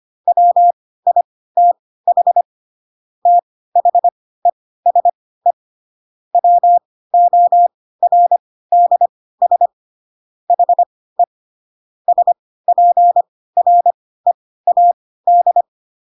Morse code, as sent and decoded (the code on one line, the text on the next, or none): .-- .. - .... / - .... ... / .-- --- .-. -.. ... / .... / ... .--. .-. .- -..
WITH THESE WORDS HE SPREAD